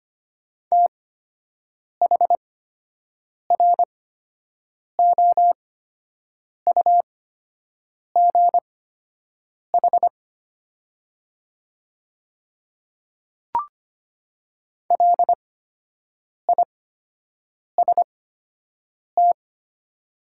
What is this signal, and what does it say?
Morse code, heard as a keyed tone, about 25 wpm, spaced as for 7 wpm, 700 Hz. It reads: THROUGH LIST
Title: 8x Character Spacing - Top 500 Words - 25wpm